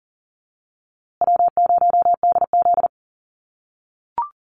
Morse code, heard as a keyed tone, about 40 wpm, 700 Hz. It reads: W0DZ
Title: USA Call Signs Repeated 2x 40wpm